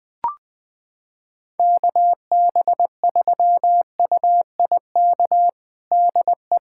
Morse code, heard as a keyed tone, about 20 wpm, 700 Hz. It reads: KB3UIK DE
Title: USA Callsign DE Callsign - 6 Characters - 20wpm